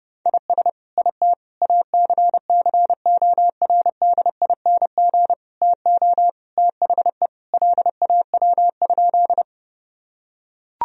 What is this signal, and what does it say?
Morse code, heard as a keyed tone, about 32 wpm, 700 Hz. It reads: IS IT ACCORDING TO THE LAW?
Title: Sentences from Top 1000 Words - Review Part 4 - 30wpm